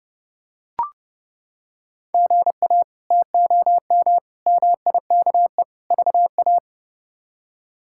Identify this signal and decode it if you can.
Morse code, heard as a keyed tone, about 30 wpm, 700 Hz. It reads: GA TOM MIKE VA
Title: K1USN SST QSO Practice - 30wpm